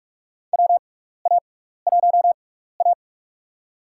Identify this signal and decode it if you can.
Morse code, heard as a keyed tone, about 44 wpm, spaced as for 15 wpm, 700 Hz. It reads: WA1A
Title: USA Call Signs - 4 Characters - 6x Character Spacing - 45wpm